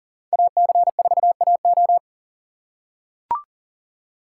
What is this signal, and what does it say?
Morse code, heard as a keyed tone, about 40 wpm, 700 Hz. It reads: AC4AO
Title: USA Call Signs - 5 Characters - 3x Character Spacing - 40wpm